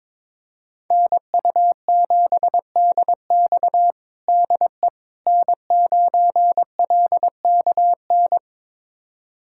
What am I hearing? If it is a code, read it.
Morse code, heard as a keyed tone, about 22 wpm, 700 Hz. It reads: NU7DX DE N9LKN
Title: USA Callsign DE Callsign - 5 Characters - 22wpm